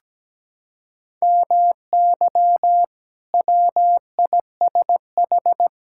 Morse code, heard as a keyed tone, about 17 wpm, 700 Hz. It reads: MY WISH
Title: Sentences from Top 1600 Words 17wpm